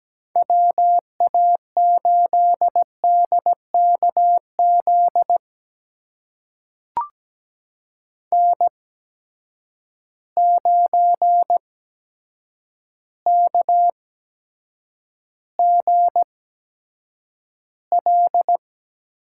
Morse code, heard as a keyed tone, about 17 wpm, spaced as for 8 wpm, 700 Hz. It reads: WA8DKZ N9KGL
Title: USA Call Signs - 8x Character Spacing - 17wpm